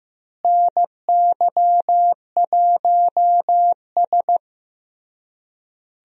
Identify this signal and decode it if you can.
Morse code, heard as a keyed tone, about 15 wpm, 700 Hz. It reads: NY1S
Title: USA Call Signs - 4 Characters - Repeated 3x - 15wpm